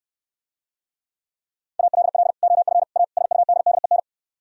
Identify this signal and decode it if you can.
Morse code, heard as a keyed tone, about 68 wpm, 700 Hz. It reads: ALL OF A SUDDEN